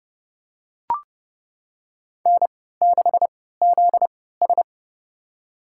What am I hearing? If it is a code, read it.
Morse code, heard as a keyed tone, about 30 wpm, spaced as for 17 wpm, 700 Hz. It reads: N6ZS
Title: USA Call Signs - 4 Characters - 3x Character Spacing - 30wpm